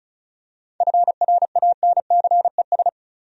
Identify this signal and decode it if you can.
Morse code, heard as a keyed tone, about 36 wpm, 700 Hz. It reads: FRANCES
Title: USA Common Names - 35wpm